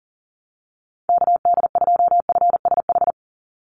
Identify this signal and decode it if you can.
Morse code, heard as a keyed tone, about 40 wpm, 700 Hz. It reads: KD2FSH
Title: Call Signs Repeated 3x - 40wpm